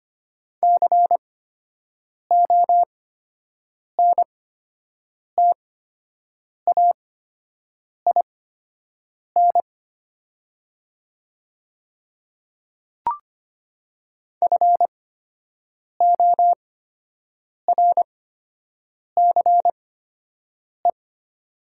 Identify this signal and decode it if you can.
Morse code, heard as a keyed tone, about 25 wpm, spaced as for 7 wpm, 700 Hz. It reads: CONTAIN FORCE
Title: 8x Character Spacing - Top 500 Words - 25wpm